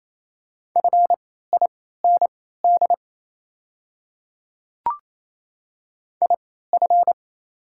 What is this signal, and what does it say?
Morse code, heard as a keyed tone, about 27 wpm, spaced as for 16 wpm, 690 Hz. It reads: FIND IF
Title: 3x Character Spacing - Top 100 Words - 28wpm